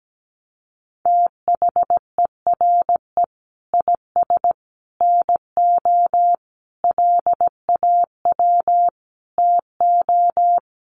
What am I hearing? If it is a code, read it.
Morse code, heard as a keyed tone, about 17 wpm, 700 Hz. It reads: THERE IS NO LAW TO
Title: Sentences from Top 1100 Words 17wpm